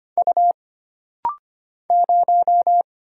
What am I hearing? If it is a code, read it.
Morse code, heard as a keyed tone, about 25 wpm, 700 Hz. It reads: U 0
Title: Single Letter-Number Alt - Rapid-Fire 25wpm